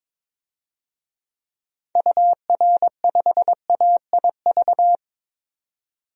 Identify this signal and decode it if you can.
Morse code, heard as a keyed tone, about 22 wpm, 700 Hz. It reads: UR5AIV
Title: DXCC-100 Call Signs - 6 Characters - 22wpm